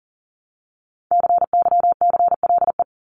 Morse code, heard as a keyed tone, about 40 wpm, 700 Hz. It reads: CYCLE
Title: Top 751-1000 Words 40wpm